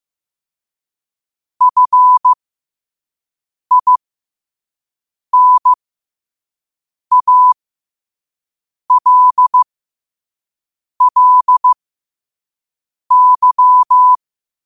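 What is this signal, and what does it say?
Morse code, heard as a keyed tone, about 15 wpm, spaced as for 5 wpm, 1000 Hz. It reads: FINALLY
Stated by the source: Morse code quote practice at 5 words per minute.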